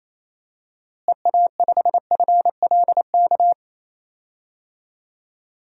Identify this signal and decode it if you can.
Morse code, heard as a keyed tone, about 28 wpm, 700 Hz. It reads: EA5FLK
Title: DXCC-100 Call Signs - 6 Characters - Repeated 3x - 28wpm